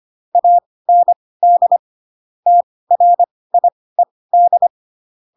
Morse code, heard as a keyed tone, about 25 wpm, spaced as for 18 wpm, 700 Hz. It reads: AND TRIED
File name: Pinocchio ch3 12 25 ik1hge